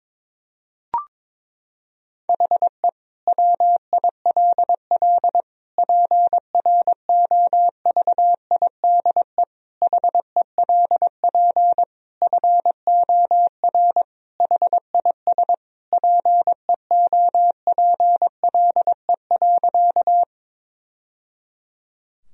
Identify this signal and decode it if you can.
Morse code, heard as a keyed tone, about 22 wpm, 700 Hz. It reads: HE WILL PROVIDE HELP FOR HIS PEOPLE.
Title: Sentences from Top 1000 Words - Review Part 3 - 22wpm